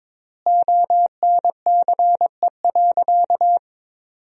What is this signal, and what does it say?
Morse code, heard as a keyed tone, about 22 wpm, 700 Hz. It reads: ONCE.
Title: 3x Word-Spacing - Sentences from Top 400 Words - 22wpm